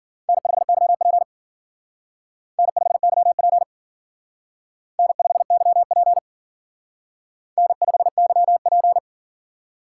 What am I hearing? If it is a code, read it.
Morse code, heard as a keyed tone, about 51 wpm, 700 Hz. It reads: N5YP N5YP N5YP N5YP